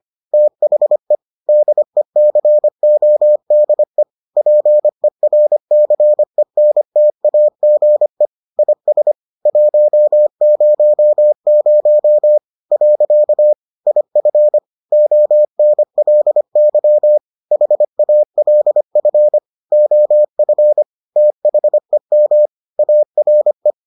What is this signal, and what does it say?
Morse code, heard as a keyed tone, about 25 wpm, 600 Hz. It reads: THE DECODE PERCENTAGE IS 100 . IF ONLY HALF OF THEM ARE